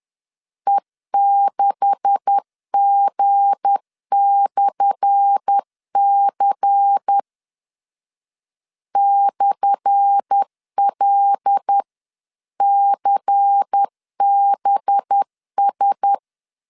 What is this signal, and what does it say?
Morse code, heard as a keyed tone, about 10 wpm, 800 Hz. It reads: E6G/C /LCBS
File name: LEZIONI / 10 SIGNS / 10A SLASH 10wpm 2808